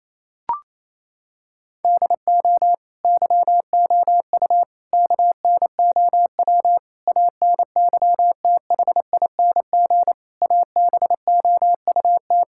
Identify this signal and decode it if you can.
Morse code, heard as a keyed tone, about 28 wpm, 700 Hz. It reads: DO YOU KNOW ANYTHING ABOUT
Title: Sentences from Top 1600 Words 28wpm